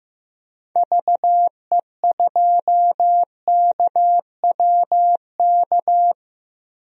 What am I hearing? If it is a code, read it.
Morse code, heard as a keyed tone, about 15 wpm, 700 Hz. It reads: VE2KWK
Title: DXCC-100 Call Signs - 6 Characters - 15wpm